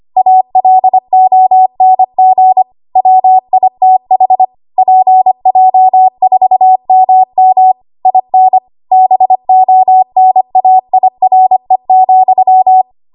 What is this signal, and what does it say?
Morse code, heard as a keyed tone, about 25 wpm, 750 Hz. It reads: ALONG WITH PJ4MM IN BONAIRE,
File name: Morse / 25WPM